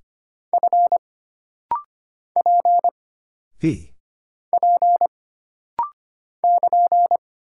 Morse code, heard as a keyed tone, about 25 wpm, 700 Hz. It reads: F P P <KN>
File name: Single Character (Letter:Number:Punctuation:Prosign) - Rapid-Fire - 25wpm